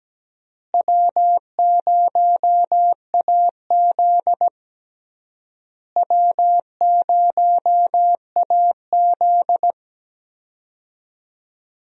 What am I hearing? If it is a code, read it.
Morse code, heard as a keyed tone, about 17 wpm, 700 Hz. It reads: W0AZ W0AZ
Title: USA Call Signs - 4 Characters - Repeated 3x - 17wpm